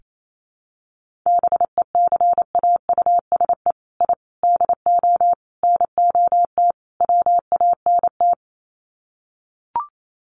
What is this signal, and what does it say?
Morse code, heard as a keyed tone, about 28 wpm, 700 Hz. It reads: BECAUSE I DO NOT WANT
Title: Sets of 5 Words Encore 28wpm